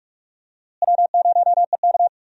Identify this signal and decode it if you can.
Morse code, heard as a keyed tone, about 42 wpm, 700 Hz. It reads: W0EK
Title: USA Call Signs - 4 Characters - Repeated 3x - 45wpm